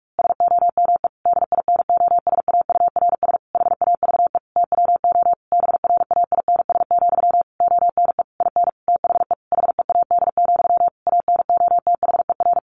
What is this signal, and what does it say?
Morse code, heard as a keyed tone, about 44 wpm, 700 Hz. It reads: SOME DINOSAURS HAVE TWO BRAINS, ONE IN THE HEAD, ANOTHER
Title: A Sound Of Thunder 45wpm